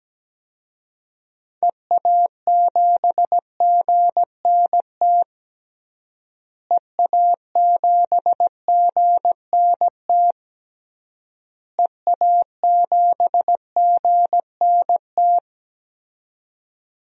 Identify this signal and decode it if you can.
Morse code, heard as a keyed tone, about 17 wpm, 700 Hz. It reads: EA7GNT EA7GNT EA7GNT
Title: DXCC-100 Call Signs - 6 Characters - Repeated 3x - 17wpm